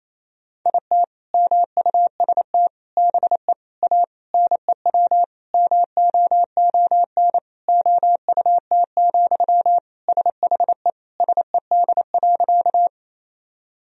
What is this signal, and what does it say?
Morse code, heard as a keyed tone, about 28 wpm, 700 Hz. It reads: IT MUST BE A NEW MOON OUT, SHE SED.
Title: Sentences from Top 500 Words and Abbreviations 28wpm